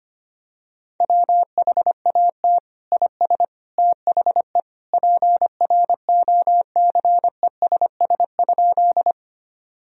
Morse code, heard as a keyed tone, about 25 wpm, 700 Hz. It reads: WHAT IS THE PROCESS?
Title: Sentences from Top 1000 Words - Review Part 3 - 25wpm